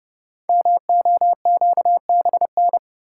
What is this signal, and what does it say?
Morse code, heard as a keyed tone, about 28 wpm, 700 Hz. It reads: MOQBN
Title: Code Groups - 5 Letters - 30wpm